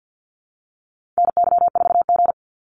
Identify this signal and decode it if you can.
Morse code, heard as a keyed tone, about 50 wpm, 700 Hz. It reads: NY4G
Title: NA SOTA Activator Callsigns 50wpm